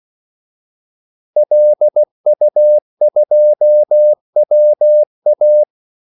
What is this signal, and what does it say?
Morse code, heard as a keyed tone, about 16 wpm, 600 Hz. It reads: LU2WA
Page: Kaldesignaler 16 wpm